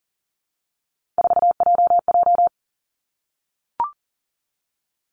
Morse code, heard as a keyed tone, about 40 wpm, 700 Hz. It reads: 4JJ
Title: Three Challenging Letter-Number Combinations 40wpm